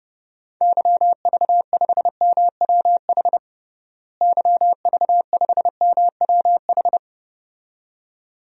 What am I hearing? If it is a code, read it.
Morse code, heard as a keyed tone, about 32 wpm, 700 Hz. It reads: YV5MWH YV5MWH
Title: DXCC-100 Call Signs - 6 Characters - Repeated 2x - 30wpm